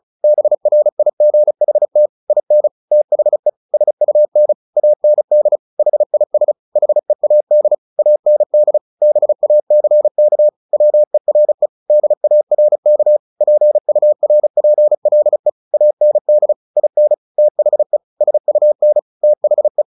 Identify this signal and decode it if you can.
Morse code, heard as a keyed tone, about 34 wpm, 600 Hz. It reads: BRIGHT IN THE SUN AND HIS HEAD AND BACK WERE DARK PURPLE AND IN THE SUN THE
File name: the old man and the sea / the old man and the sea 1 35wpm